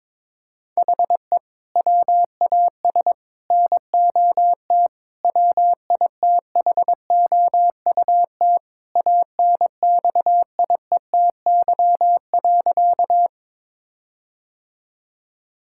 Morse code, heard as a keyed tone, about 22 wpm, 700 Hz. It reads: HE WAS NOT WITHOUT ANXIETY.
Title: Sentences from Top 1600 Words 22wpm